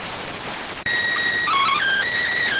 SSTV (slow-scan television), PD 290.